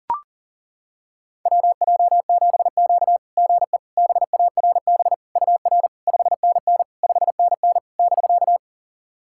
Morse code, heard as a keyed tone, about 40 wpm, 700 Hz. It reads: WJ7Q GE BARB UR 5NN 5NN <BK>
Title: SOTA QSO Practice 40wpm